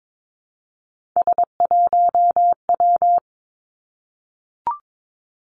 Morse code, heard as a keyed tone, about 22 wpm, 700 Hz. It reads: S1W
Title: Three Challenging Letter-Number Combinations 22wpm